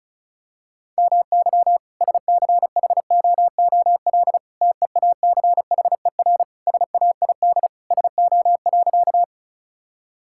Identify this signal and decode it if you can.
Morse code, heard as a keyed tone, about 36 wpm, 700 Hz. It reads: MY SCHOOL TEACHER SAID SO.
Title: Sentences from Top 1000 Words - Review Part 6 - 35wpm